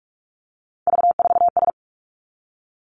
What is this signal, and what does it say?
Morse code, heard as a keyed tone, about 46 wpm, 700 Hz. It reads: V4S
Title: Three Challenging Letter-Number Combinations 45wpm